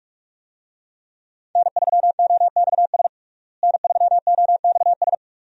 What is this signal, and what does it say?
Morse code, heard as a keyed tone, about 46 wpm, 700 Hz. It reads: N3OXS N3OXS
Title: USA Call Signs - 5 Characters - Repeated 2x - 45wpm